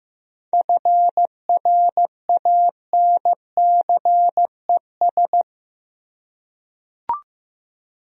Morse code, heard as a keyed tone, about 15 wpm, 700 Hz. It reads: FRANCES
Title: USA Common Names - 15wpm